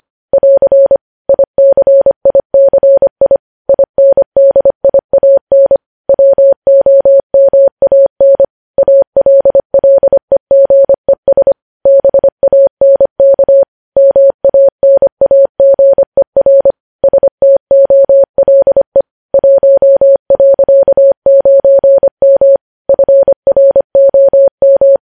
Morse code, heard as a keyed tone, about 25 wpm, 550 Hz. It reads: <AR> ICICI INDIAN WOMAN ALLEGES BANK MANAGER STOLE 1.9M FROM
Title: News Headlines at 25